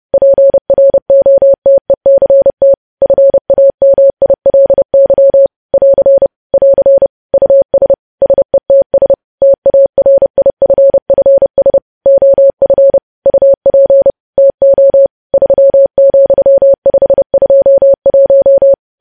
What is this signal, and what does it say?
Morse code, heard as a keyed tone, about 30 wpm, 550 Hz. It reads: PROTECT FAMILY <AR> <AR> US SETS TARIFFS OF UP TO 3,521
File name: podcast / News headlines-1x-30x30